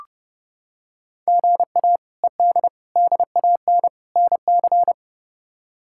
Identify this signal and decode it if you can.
Morse code, heard as a keyed tone, about 30 wpm, 700 Hz. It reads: GA ED DAN NC
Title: K1USN SST QSO Practice - 30wpm